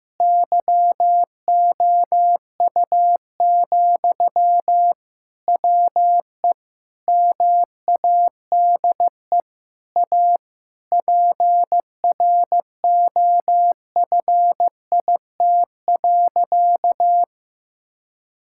Morse code, heard as a keyed tone, about 15 wpm, 700 Hz. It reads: YOU, WE MADE A PROFIT.
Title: Sentences from Top 1000 Words - Review Part 5 - 15wpm